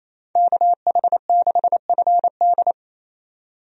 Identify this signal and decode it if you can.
Morse code, heard as a keyed tone, about 28 wpm, 700 Hz. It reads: KH6FD
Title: USA Call Signs - 5 Characters - 28wpm